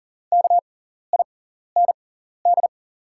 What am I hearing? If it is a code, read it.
Morse code, heard as a keyed tone, about 40 wpm, spaced as for 14 wpm, 700 Hz. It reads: KIND
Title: 6x Character Spacing - Top 200 Words - 40wpm